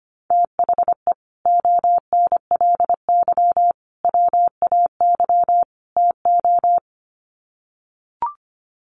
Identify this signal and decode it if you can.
Morse code, heard as a keyed tone, about 25 wpm, 700 Hz. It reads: THE ONLY WAY TO
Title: Sets of 4 Words 25wpm